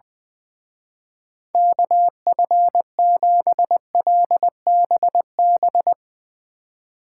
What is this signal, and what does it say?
Morse code, heard as a keyed tone, about 20 wpm, 700 Hz. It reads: KF7LBB
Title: Speed Racing - USA 2x3 Call Signs 20wpm